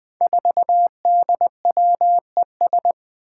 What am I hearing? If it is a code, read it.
Morse code, heard as a keyed tone, about 20 wpm, 700 Hz. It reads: 4DWES
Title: Five Challenging Letter-Number Combinations 20wpm